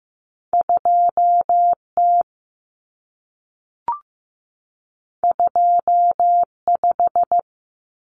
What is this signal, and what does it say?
Morse code, heard as a keyed tone, about 15 wpm, 700 Hz. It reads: 2T 25
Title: Common Cut Numbers - 2-Digit 15wpm